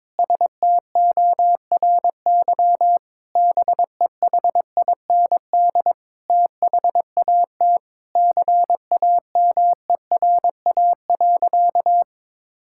Morse code, heard as a keyed tone, about 22 wpm, 700 Hz. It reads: STORY BEHIND THAT CAMERA.
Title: Sentences from Top 1000 Words - Review Part 5 - 22wpm